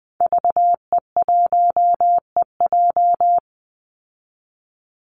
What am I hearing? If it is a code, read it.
Morse code, heard as a keyed tone, about 20 wpm, 700 Hz. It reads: VE1EJ
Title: Call Signs Repeated 3x - 20wpm